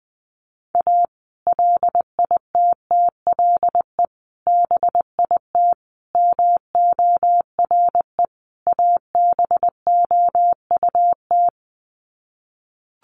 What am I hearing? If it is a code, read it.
Morse code, heard as a keyed tone, about 19 wpm, 700 Hz. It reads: A LITTLE BIT MORE ABOUT
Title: Sets of 5 Words Encore2 20wpm